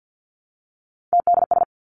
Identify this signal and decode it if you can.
Morse code, heard as a keyed tone, about 50 wpm, 700 Hz. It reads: TDS
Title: Three Challenging Letter-Number Combinations 50wpm